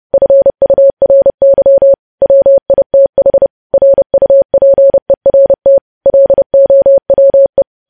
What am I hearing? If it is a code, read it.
Morse code, heard as a keyed tone, about 30 wpm, 550 Hz. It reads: FURY WITH RUPERT LOWE
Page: news headlines in morse code at 30 wpm